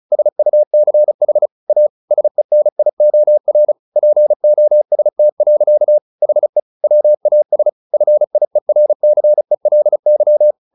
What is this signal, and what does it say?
Morse code, heard as a keyed tone, about 34 wpm, 600 Hz. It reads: SUCH A SENIOR POST. HE WAS FIERCELY